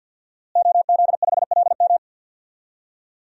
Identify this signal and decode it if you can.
Morse code, heard as a keyed tone, about 48 wpm, 700 Hz. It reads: OZ5LM
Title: DXCC-100 Call Signs - 5 Characters - Repeated 3x - 50wpm